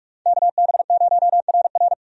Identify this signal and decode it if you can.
Morse code, heard as a keyed tone, about 46 wpm, 700 Hz. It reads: KB0RR
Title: USA Call Signs - 5 Characters - 45wpm